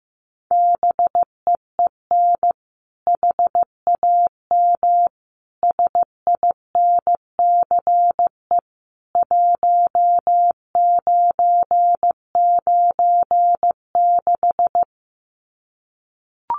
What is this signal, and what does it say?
Morse code, heard as a keyed tone, about 15 wpm, 700 Hz. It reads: BEEN HAM SINCE 1996
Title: Common QSO Phrases 15wpm